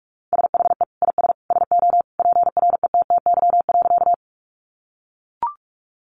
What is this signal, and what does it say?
Morse code, heard as a keyed tone, about 44 wpm, 700 Hz. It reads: SHE IS SO PRETTY.
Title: Sentences from Top 1000 Words 45wpm